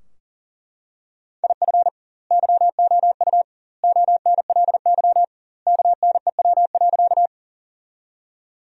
Morse code, heard as a keyed tone, about 38 wpm, 700 Hz. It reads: IF YOU ONLY KNEW.
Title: 2x Word-Spacing - Sentences from Top 400 Words - 40wpm